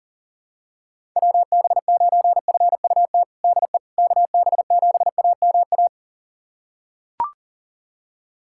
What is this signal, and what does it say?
Morse code, heard as a keyed tone, about 40 wpm, 700 Hz. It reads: WB9FUT DE KB7AMA
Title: USA Callsign DE Callsign - 6 Characters - 40wpm